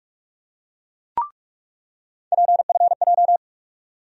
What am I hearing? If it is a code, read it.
Morse code, heard as a keyed tone, about 46 wpm, 700 Hz. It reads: PFJ
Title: USA Call Sign Postfixes - 3 Letters 45wpm